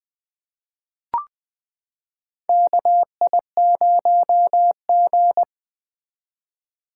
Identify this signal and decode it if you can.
Morse code, heard as a keyed tone, about 21 wpm, 700 Hz. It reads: KI0G
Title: 2023 SOTA Chaser Call Signs - 20wpm